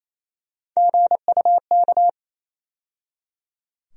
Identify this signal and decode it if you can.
Morse code, heard as a keyed tone, about 28 wpm, 700 Hz. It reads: GUK
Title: USA Call Sign Postfixes - 3 Letters 28wpm